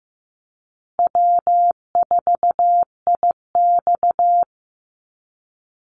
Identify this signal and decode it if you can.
Morse code, heard as a keyed tone, about 15 wpm, 700 Hz. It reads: W4IX
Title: Call Signs Repeated 2x - 15wpm